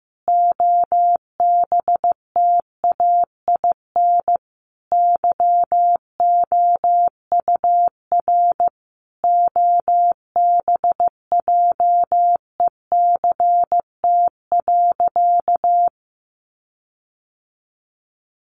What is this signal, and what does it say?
Morse code, heard as a keyed tone, about 15 wpm, 700 Hz. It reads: OBTAIN YOUR OBJECT.
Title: Sentences from Top 1300 Words 15wpm